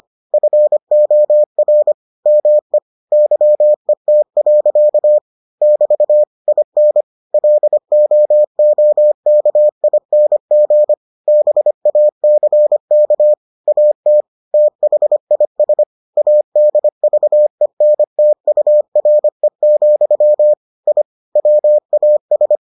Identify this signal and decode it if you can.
Morse code, heard as a keyed tone, about 25 wpm, 600 Hz. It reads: FOR ME YET. <BT> IN LOOKING BACK AT THIS ADVENTURE, I WAS